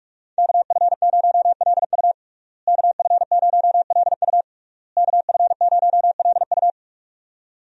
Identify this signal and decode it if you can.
Morse code, heard as a keyed tone, about 46 wpm, 700 Hz. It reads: KF0LU KF0LU KF0LU
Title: USA Call Signs - 5 Characters - Repeated 3x - 45wpm